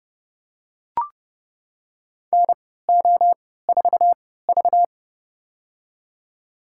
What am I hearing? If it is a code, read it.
Morse code, heard as a keyed tone, about 29 wpm, spaced as for 17 wpm, 700 Hz. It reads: NO4V
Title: USA Call Signs - 4 Characters - 3x Character Spacing - 30wpm